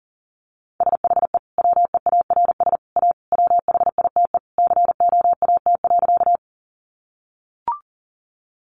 Morse code, heard as a keyed tone, about 40 wpm, 700 Hz. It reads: SHE WEARS A WHITE COAT.
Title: Additional Easy Phrases 40wpm